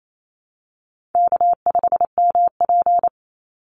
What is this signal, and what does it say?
Morse code, heard as a keyed tone, about 28 wpm, 700 Hz. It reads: K5MP
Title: Call Signs Repeated 3x - 28wpm